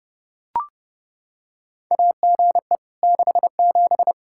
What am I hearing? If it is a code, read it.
Morse code, heard as a keyed tone, about 32 wpm, 700 Hz. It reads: AGE 67
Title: Practice QSO - Exchange 3 - More Info - 30wpm